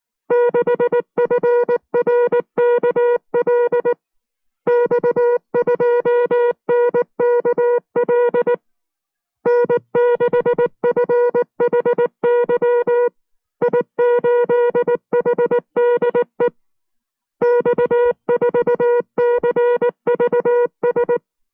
Morse code, heard as a keyed tone, about 19 wpm, 480 Hz. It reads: 6FRKL X2NKL N6FHY I8HDE X4CVS